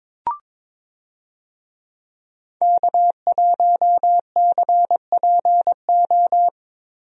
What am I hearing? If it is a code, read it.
Morse code, heard as a keyed tone, about 21 wpm, 700 Hz. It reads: K1CPO
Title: USA Call Signs - 5 Characters - Repeated 2x - 22wpm